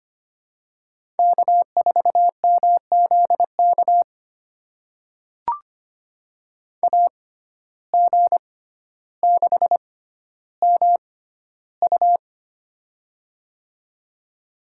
Morse code, heard as a keyed tone, about 25 wpm, spaced as for 13 wpm, 700 Hz. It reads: K4MZK AG6MU
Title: USA Call Signs - 5 Characters - 6x Character Spacing - 25wpm